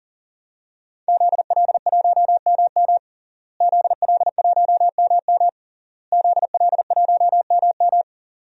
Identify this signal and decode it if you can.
Morse code, heard as a keyed tone, about 40 wpm, 700 Hz. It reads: ZL1MM ZL1MM ZL1MM
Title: DXCC-100 Call Signs - 5 Characters - Repeated 3x - 40wpm